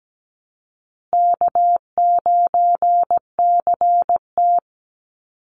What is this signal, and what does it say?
Morse code, heard as a keyed tone, about 17 wpm, 700 Hz. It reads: K9CT
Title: Easy Call Signs Repeated 3x - 17wpm